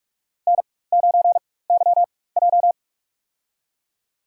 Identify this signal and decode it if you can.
Morse code, heard as a keyed tone, about 45 wpm, spaced as for 21 wpm, 700 Hz. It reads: N9YJ
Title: USA Call Signs - 4 Characters - 4x Character Spacing - 45wpm